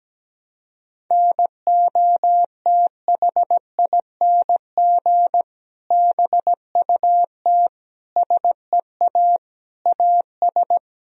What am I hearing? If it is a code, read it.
Morse code, heard as a keyed tone, about 17 wpm, 700 Hz. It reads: NOTHING BUT SEA AS
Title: Sentences from Top 1000 Words - Review Part 1 - 17wpm